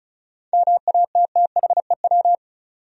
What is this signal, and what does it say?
Morse code, heard as a keyed tone, about 34 wpm, 700 Hz. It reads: MATTHEW
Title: USA Common Names - 35wpm